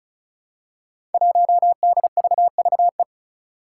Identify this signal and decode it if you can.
Morse code, heard as a keyed tone, about 34 wpm, 700 Hz. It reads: 1DVVE
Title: Five Challenging Letter-Number Combinations 35wpm